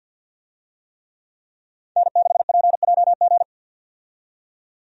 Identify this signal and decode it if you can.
Morse code, heard as a keyed tone, about 50 wpm, 700 Hz. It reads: N6PJG
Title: USA Call Signs - 5 Characters - 50wpm